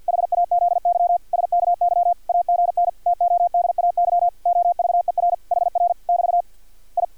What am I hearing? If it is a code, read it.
Morse code, heard as a keyed tone, about 50 wpm, 700 Hz, with radio noise. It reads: HAZY SKY AGN TODAY OVER HR <BT> I
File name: archived / CRSnet